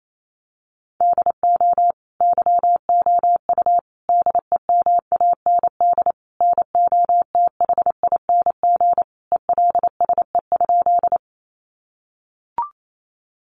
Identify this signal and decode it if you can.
Morse code, heard as a keyed tone, about 28 wpm, 700 Hz. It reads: DO YOU DEMAND NOTHING ELSE?
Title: Sentences from Top 800 Words 28wpm